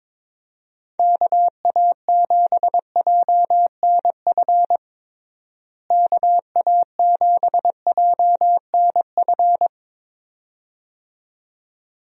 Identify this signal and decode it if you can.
Morse code, heard as a keyed tone, about 22 wpm, 700 Hz. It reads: KA7JNF KA7JNF
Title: USA Call Signs - 6 Characters - Repeated 2x - 22wpm